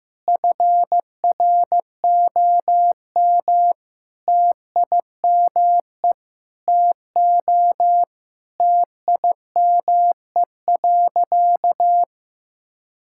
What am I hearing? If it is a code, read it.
Morse code, heard as a keyed tone, about 16 wpm, 700 Hz. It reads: FROM TIME TO TIME.
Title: Sentences from Top 1000 Words - Review Part 6 - 15wpm